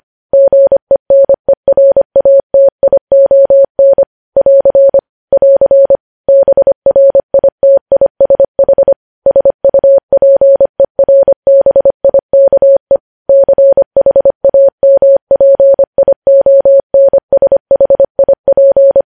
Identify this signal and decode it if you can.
Morse code, heard as a keyed tone, about 25 wpm, 550 Hz. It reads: GENERATION <AR> <AR> BRITISH SUPERBIKE CHAMPIONSHIP
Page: News Headlines in Morse Code at 25 WPM